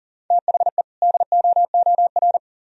Morse code, heard as a keyed tone, about 46 wpm, 700 Hz. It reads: THE DOOR
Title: Speed-Racing - Sets of 2 Words 40wpm